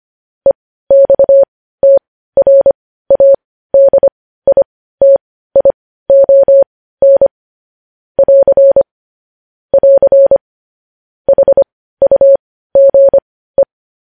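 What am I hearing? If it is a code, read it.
Morse code, heard as a keyed tone, about 25 wpm, spaced as for 15 wpm, 550 Hz. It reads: EXTRADITION <AR> <AR> HUGE
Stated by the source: Morse code news headlines sent at 15, opening with vvv.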